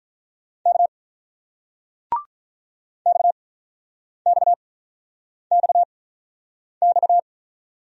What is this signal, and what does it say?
Morse code, heard as a keyed tone, about 50 wpm, spaced as for 9 wpm, 700 Hz. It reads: K XXXX